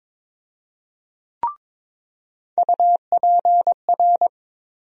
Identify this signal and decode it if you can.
Morse code, heard as a keyed tone, about 22 wpm, 700 Hz. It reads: UPR
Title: USA Call Sign Postfixes - 3 Letters 22wpm